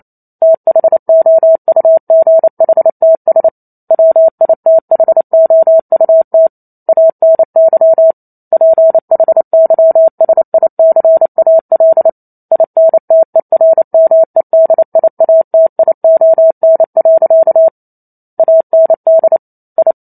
Morse code, heard as a keyed tone, about 28 wpm, 650 Hz. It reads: THOUGHTS WITHOUT ANY PHYSICAL INTERMEDIATION. AND I
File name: Chapter 2 - 28 WPM